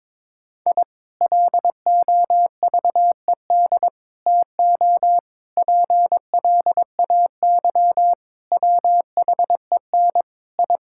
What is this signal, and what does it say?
Morse code, heard as a keyed tone, about 22 wpm, 700 Hz. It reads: I LOVED TO PLAY WHEN I
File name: practice / 22wpm / Sentences with 4-Word Phrases - 22wpm